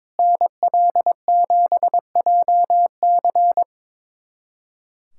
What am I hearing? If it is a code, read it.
Morse code, heard as a keyed tone, about 22 wpm, 700 Hz. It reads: NL7JC
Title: USA Callsign DE Callsign 22wpm